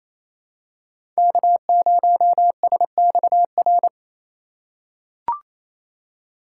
Morse code, heard as a keyed tone, about 28 wpm, 700 Hz. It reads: K0SXR
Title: USA Call Signs - 5 Characters - 2x Character Spacing - 28wpm